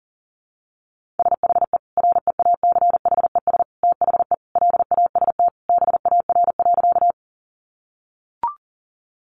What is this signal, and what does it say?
Morse code, heard as a keyed tone, about 40 wpm, 700 Hz. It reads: SHE REACHES THE LAST BAR.
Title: Easy Phrases 40wpm